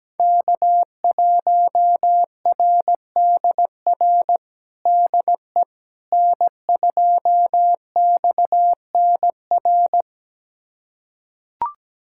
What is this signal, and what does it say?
Morse code, heard as a keyed tone, about 17 wpm, 700 Hz. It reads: K1RDR DE N2XNR
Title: USA Callsign DE Callsign - 5 Characters - 17wpm